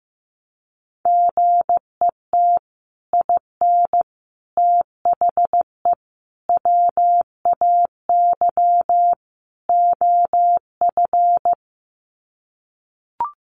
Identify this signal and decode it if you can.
Morse code, heard as a keyed tone, about 15 wpm, 700 Hz. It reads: GET IN THE WAY OF
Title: Sets of 5 Words Encore 15wpm